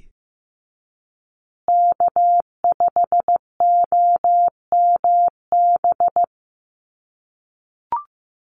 Morse code, heard as a keyed tone, about 15 wpm, 700 Hz. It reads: K5OMB
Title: USA Call Signs Repeated 2x 15wpm